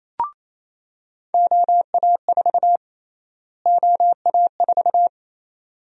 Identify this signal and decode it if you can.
Morse code, heard as a keyed tone, about 28 wpm, 700 Hz. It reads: OA4 OA4
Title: USA Call Sign Prefixes - Repeated 2x - 28wpm